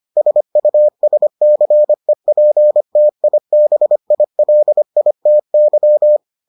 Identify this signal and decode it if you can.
Morse code, heard as a keyed tone, about 25 wpm, 600 Hz. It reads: SUSCEPTIBILITY